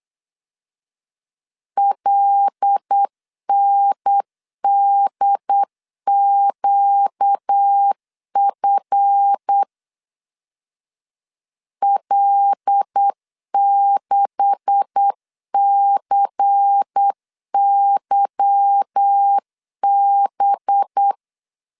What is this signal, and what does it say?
Morse code, heard as a keyed tone, about 8 wpm, 800 Hz. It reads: LNDQF L6CYB
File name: CW / DET / 1 9 1500C S C 8wpm 5653